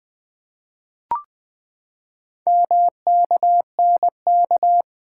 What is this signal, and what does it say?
Morse code, heard as a keyed tone, about 20 wpm, 700 Hz. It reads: MKNK
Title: Four Letter-Number Combinations - 20wpm